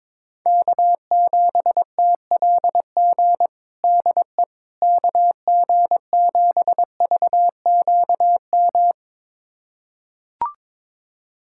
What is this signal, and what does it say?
Morse code, heard as a keyed tone, about 22 wpm, 700 Hz. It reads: K7TLG DE KG7VQM
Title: USA Callsign DE Callsign 22wpm